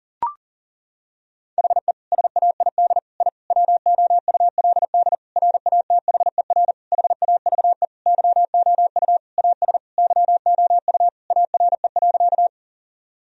Morse code, heard as a keyed tone, about 37 wpm, 700 Hz. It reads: HE SAID I WOULD RATHER HAVE YOU AS YOU ARE.
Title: Sentences from Top 1000 Words - Review Part 4 - 40wpm